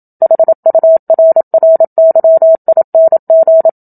Morse code, heard as a keyed tone, about 27 wpm, 650 Hz. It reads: HURRYING
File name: Book Two / 01-Chapter 1 - 27 WPM